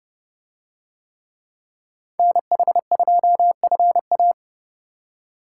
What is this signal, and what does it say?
Morse code, heard as a keyed tone, about 30 wpm, 700 Hz. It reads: NH2FA